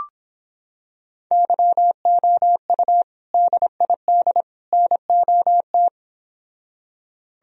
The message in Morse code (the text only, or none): YOU DID NOT